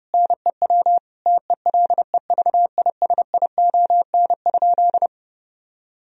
Morse code, heard as a keyed tone, about 32 wpm, 700 Hz. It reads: NEW TELEVISION?
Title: Sentences from Top 1000 Words - Review Part 4 - 30wpm